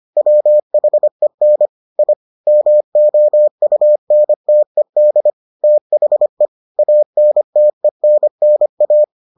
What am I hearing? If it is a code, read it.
Morse code, heard as a keyed tone, about 25 wpm, 600 Hz. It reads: WHEN I MOUNTED THE ANTENNA